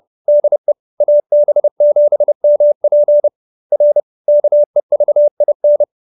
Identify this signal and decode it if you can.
Morse code, heard as a keyed tone, about 30 wpm, 590 Hz. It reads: DE AB7MP R KEVIN